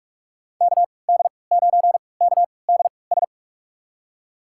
Morse code, heard as a keyed tone, about 46 wpm, spaced as for 26 wpm, 690 Hz. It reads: KD9KDS
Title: USA Call Signs - 6 Characters - 3x Character Spacing - 45wpm